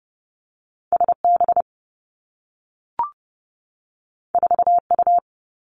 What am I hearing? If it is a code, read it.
Morse code, heard as a keyed tone, about 30 wpm, 700 Hz. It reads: S B 4 U
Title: Two Challenging Letter-Number Combinations 30wpm